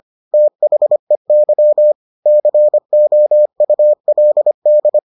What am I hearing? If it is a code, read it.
Morse code, heard as a keyed tone, about 25 wpm, 600 Hz. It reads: THEY COULD